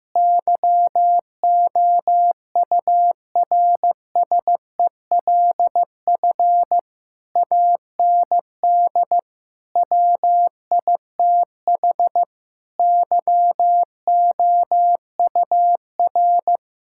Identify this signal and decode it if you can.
Morse code, heard as a keyed tone, about 15 wpm, 700 Hz. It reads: YOURSELF AND WITH YOUR
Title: Sentences from Top 1000 Words - Review Part 6 - 15wpm